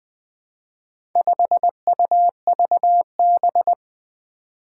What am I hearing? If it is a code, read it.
Morse code, heard as a keyed tone, about 20 wpm, 700 Hz. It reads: 5UVB
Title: Four Challenging Letter-Number Combinations 20wpm